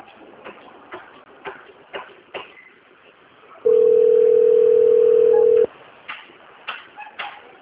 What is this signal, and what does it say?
Ringback tone (North America), 440 + 480 Hz.